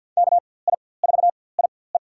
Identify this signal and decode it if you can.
Morse code, heard as a keyed tone, about 47 wpm, spaced as for 23 wpm, 700 Hz. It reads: KI4IE
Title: USA Call Signs - 5 Characters - 4x Character Spacing - 50wpm